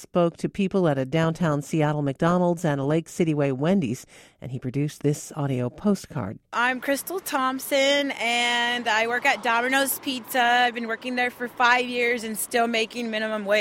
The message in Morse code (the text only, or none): none